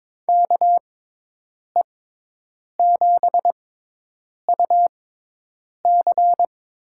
Morse code, heard as a keyed tone, about 22 wpm, spaced as for 8 wpm, 700 Hz. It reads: KE7UC
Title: USA Call Signs - 6x Character Spacing - 22wpm